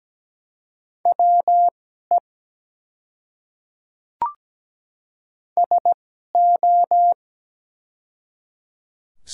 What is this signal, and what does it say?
Morse code, heard as a keyed tone, about 17 wpm, spaced as for 12 wpm, 700 Hz. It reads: WE SO
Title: 2x Character Spacing - Top 200 Words - 17wpm